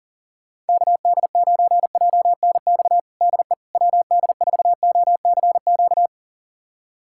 Morse code, heard as a keyed tone, about 40 wpm, 700 Hz. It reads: KD9JNX DE WD4OCQ
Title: USA Callsign DE Callsign 40wpm